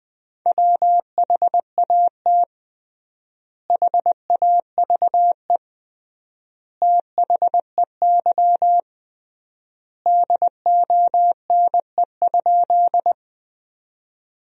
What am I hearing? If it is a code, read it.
Morse code, heard as a keyed tone, about 20 wpm, 700 Hz. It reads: WHAT HAVE THEY DONE?
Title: 3x Word-Spacing - Sentences from Top 500 Words - 20wpm